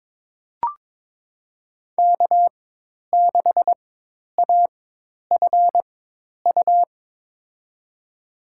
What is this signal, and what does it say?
Morse code, heard as a keyed tone, about 22 wpm, spaced as for 10 wpm, 700 Hz. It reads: K6AFU
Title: USA Call Signs - 5 Characters - 4x Character Spacing - 22wpm